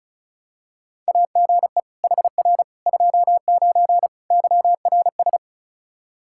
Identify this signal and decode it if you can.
Morse code, heard as a keyed tone, about 36 wpm, 700 Hz. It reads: AGE HR 29 YRS
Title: Practice QSO - Exchange 3 - More Info - 35wpm